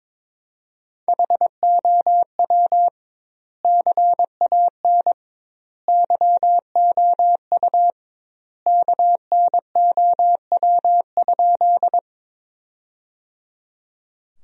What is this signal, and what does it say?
Morse code, heard as a keyed tone, about 22 wpm, 700 Hz. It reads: HOW CAN YOU KNOW?
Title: 2x Word-Spacing - Sentences from Top 100 Words - 22wpm